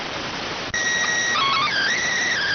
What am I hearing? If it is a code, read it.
SSTV (slow-scan television), PD 290.